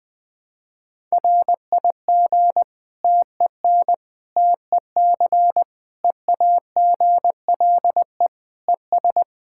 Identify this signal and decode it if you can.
Morse code, heard as a keyed tone, about 21 wpm, 700 Hz. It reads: RIG TEN TEC EAGLE ES
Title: Practice QSO - Exchange 2 - Info - 20wpm